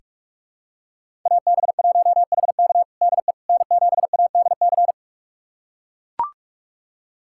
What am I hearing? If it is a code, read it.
Morse code, heard as a keyed tone, about 42 wpm, 700 Hz. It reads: AB1HK DE N7ADC
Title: USA Callsign DE Callsign - 5 Characters - 45wpm